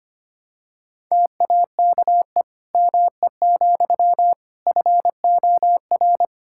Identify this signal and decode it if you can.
Morse code, heard as a keyed tone, about 25 wpm, 700 Hz. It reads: TAKE ME, FOR
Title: Sentences from Top 1000 Words - Review Part 5 - 25wpm